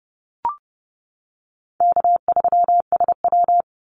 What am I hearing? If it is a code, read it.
Morse code, heard as a keyed tone, about 30 wpm, 700 Hz. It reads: K3SW
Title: Easy Call Signs Repeated 3x - 30wpm